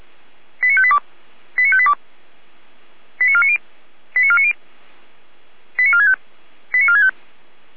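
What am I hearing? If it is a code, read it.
Five-tone selective call (ZVEI), heard as quick runs of five tones, tones of about 70 ms. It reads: 89582 89582 89490 89490 89465 89465
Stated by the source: ZVEI five-tone selective call received by SDR.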